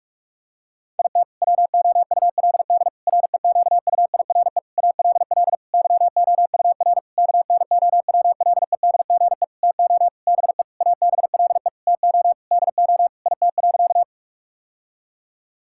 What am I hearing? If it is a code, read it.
Morse code, heard as a keyed tone, about 43 wpm, 700 Hz. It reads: IT WOULD REQUIRE ALL YOUR KNOWLEDGE TO BE ABLE TO DO IT.
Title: Sentences from Top 1000 Words - Review Part 6 - 45wpm